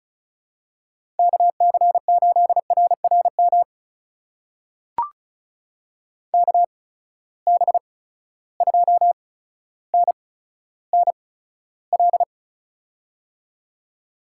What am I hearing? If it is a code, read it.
Morse code, heard as a keyed tone, about 35 wpm, spaced as for 15 wpm, 700 Hz. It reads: KC8RRM KB2NNL
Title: USA Call Signs - 6 Characters - 8x Character Spacing - 35wpm